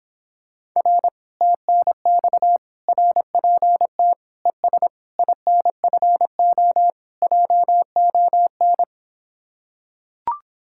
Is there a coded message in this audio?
Yes: Morse code from 0.8 to 8.8 s, heard as a keyed tone, about 26 wpm, 700 Hz. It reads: R TNX RPT ES INFO JON